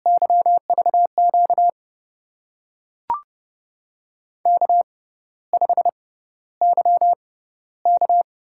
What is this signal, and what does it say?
Morse code, heard as a keyed tone, about 29 wpm, spaced as for 14 wpm, 700 Hz. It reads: YVQ K5YK
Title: USA Call Signs - 5 Characters - 6x Character Spacing - 30wpm